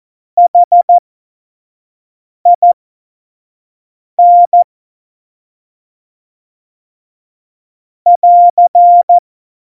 Morse code, heard as a keyed tone, about 14 wpm, spaced as for 5 wpm, 700 Hz. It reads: HIN <AR>